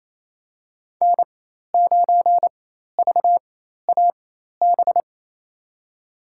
Morse code, heard as a keyed tone, about 28 wpm, spaced as for 13 wpm, 700 Hz. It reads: N9VAB
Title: USA Call Signs - 5 Characters - 4x Character Spacing - 28wpm